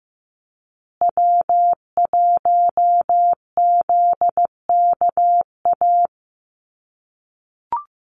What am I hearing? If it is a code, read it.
Morse code, heard as a keyed tone, about 15 wpm, 700 Hz. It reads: W1ZKA
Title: USA Call Signs Repeated 3x 15wpm